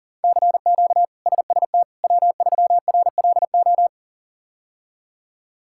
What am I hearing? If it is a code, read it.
Morse code, heard as a keyed tone, about 37 wpm, 700 Hz. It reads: CQ SST W3RLO
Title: K1USN SST QSO Practice - 40wpm